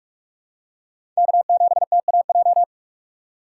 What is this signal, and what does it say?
Morse code, heard as a keyed tone, about 44 wpm, 690 Hz. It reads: K7TAJ
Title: USA Call Signs - 5 Characters - 6x Character Spacing - 45wpm